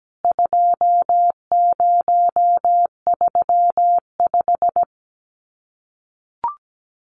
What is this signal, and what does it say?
Morse code, heard as a keyed tone, about 17 wpm, 700 Hz. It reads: 2035
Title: CWT Exchange 17wpm